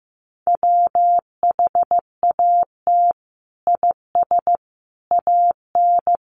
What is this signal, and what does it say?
Morse code, heard as a keyed tone, about 15 wpm, 700 Hz. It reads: WHAT IS AN
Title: Sentences from Top 1400 Words 15wpm